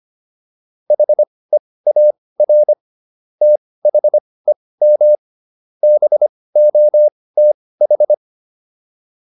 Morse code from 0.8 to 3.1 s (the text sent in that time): HEAR